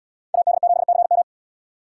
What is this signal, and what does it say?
Morse code, heard as a keyed tone, about 77 wpm, 700 Hz. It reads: AD7CM